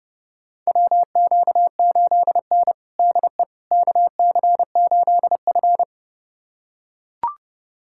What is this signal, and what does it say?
Morse code, heard as a keyed tone, about 30 wpm, 700 Hz. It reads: WQ8N DE KC8F
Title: USA Callsign DE Callsign - 4 Characters - 30wpm